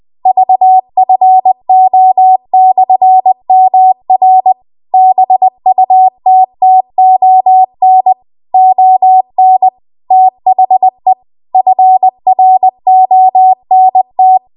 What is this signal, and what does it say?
Morse code, heard as a keyed tone, about 20 wpm, 750 Hz. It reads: VFO/MR BUTTON ON THE FRONT
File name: Archive / 20 WPM / 20WPM